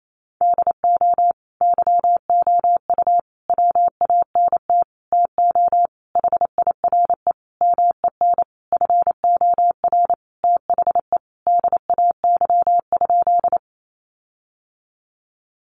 Morse code, heard as a keyed tone, about 28 wpm, 700 Hz. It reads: DO YOU WANT TO HIRE MEN FOR THE DAY?
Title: Sentences from Top 1400 Words 28wpm